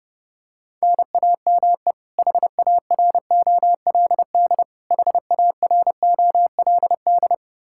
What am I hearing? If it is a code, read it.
Morse code, heard as a keyed tone, about 30 wpm, 700 Hz. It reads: NAME HAROLD HAROLD
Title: K1USN SST QSO Practice 30wpm